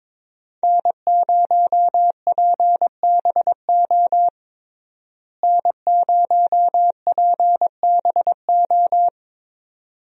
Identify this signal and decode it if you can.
Morse code, heard as a keyed tone, about 21 wpm, 700 Hz. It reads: N0PBO N0PBO
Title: USA Call Signs - 5 Characters - Repeated 2x - 22wpm